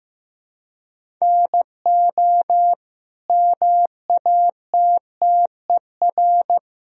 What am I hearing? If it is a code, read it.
Morse code, heard as a keyed tone, about 15 wpm, 700 Hz. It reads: NO MATTER
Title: Sentences from Top 1000 Words - Review Part 1 - 15wpm